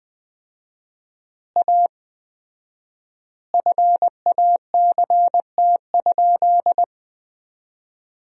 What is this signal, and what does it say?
Morse code, heard as a keyed tone, about 21 wpm, 700 Hz. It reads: A FACT?
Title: 4x Word-Spacing - Sentences from Top 500 Words - 20wpm